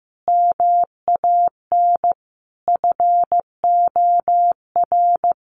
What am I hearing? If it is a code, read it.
Morse code, heard as a keyed tone, about 14 wpm, 700 Hz. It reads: MAN FOR
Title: A Sound Of Thunder 15wpm